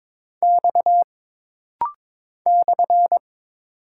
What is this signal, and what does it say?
Morse code, heard as a keyed tone, about 22 wpm, 700 Hz. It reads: X /
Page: This rapid-fire timing is geared to push your ICR (Instant Character Recognition) skills.